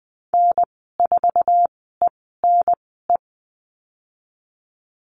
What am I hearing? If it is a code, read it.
Morse code, heard as a keyed tone, about 20 wpm, spaced as for 15 wpm, 700 Hz. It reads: N4ENE
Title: 076 Lesson 13g - Letter L - Call Signs 20wpm